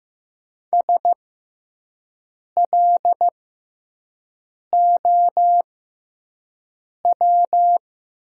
Morse code, heard as a keyed tone, about 15 wpm, spaced as for 5 wpm, 700 Hz. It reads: SLOW
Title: 6x Character Spacing - Top 500 Words - 15wpm